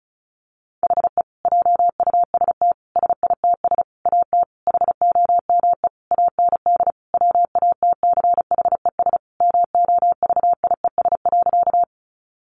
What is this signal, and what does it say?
Morse code, heard as a keyed tone, about 34 wpm, 700 Hz. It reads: HE JUST SITS AT HOME AND WATCHES MOVIES.
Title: Additional Easy Phrases 35wpm